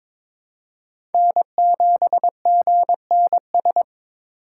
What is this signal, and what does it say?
Morse code, heard as a keyed tone, about 22 wpm, 700 Hz. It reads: N7GNS
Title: USA Call Signs - 6x Character Spacing - 22wpm